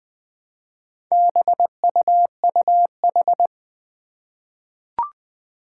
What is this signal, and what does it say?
Morse code, heard as a keyed tone, about 20 wpm, 700 Hz. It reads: BUUH
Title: Four Challenging Letter-Number Combinations 20wpm